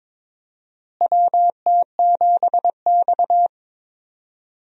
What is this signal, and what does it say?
Morse code, heard as a keyed tone, about 22 wpm, 700 Hz. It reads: WT7X
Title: USA Call Signs - 4 Characters - Repeated 3x - 22wpm